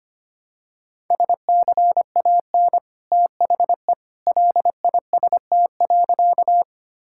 Morse code, heard as a keyed tone, about 25 wpm, 700 Hz. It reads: SCAN THE LIST.
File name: Easy Sentences with Hints - 25wpm